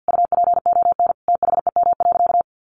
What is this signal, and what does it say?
Morse code, heard as a keyed tone, about 50 wpm, 700 Hz. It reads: UPON THEM.
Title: Sentences from Top 1400 Words 50wpm